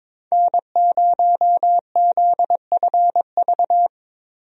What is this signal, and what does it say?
Morse code, heard as a keyed tone, about 22 wpm, 700 Hz. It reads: N0ZFV
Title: USA Call Signs - 5 Characters - Repeated 2x - 22wpm